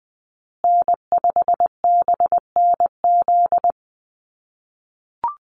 Morse code, heard as a keyed tone, about 20 wpm, 700 Hz. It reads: N5BNZ
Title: USA Call Signs Repeated 3x 20wpm